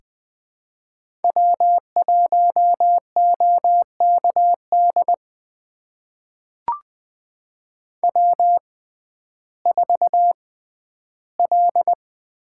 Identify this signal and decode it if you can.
Morse code, heard as a keyed tone, about 20 wpm, spaced as for 12 wpm, 700 Hz. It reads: W1OKD W4L
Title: USA Call Signs - 5 Characters - 6x Character Spacing - 20wpm